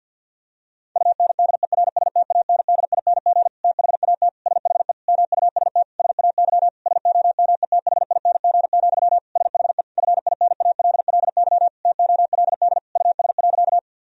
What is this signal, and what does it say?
Morse code, heard as a keyed tone, about 46 wpm, 700 Hz. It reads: UNDERSTANDING THAT SHE MUST SAY SOMETHING, SHE FINALLY TOLD US.
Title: Sentences from Top 1000 Words - Review Part 3 - 50wpm